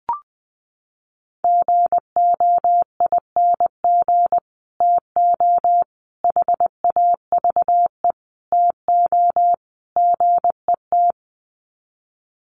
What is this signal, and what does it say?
Morse code, heard as a keyed tone, about 20 wpm, 700 Hz. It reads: GOING TO HAVE TO GET
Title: Sets of 5 Words Encore2 20wpm